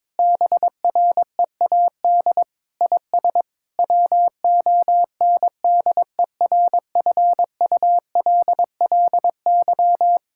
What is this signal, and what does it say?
Morse code, heard as a keyed tone, about 22 wpm, 700 Hz. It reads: BREAD IS WONDERFULLY
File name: Difficult Sentences with Hints - 22wpm